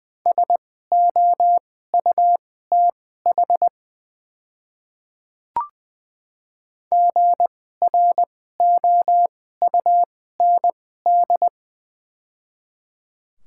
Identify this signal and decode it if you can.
Morse code, heard as a keyed tone, about 20 wpm, spaced as for 14 wpm, 690 Hz. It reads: SOUTH GROUND
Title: 2x Character Spacing - Top 500 Words - 20wpm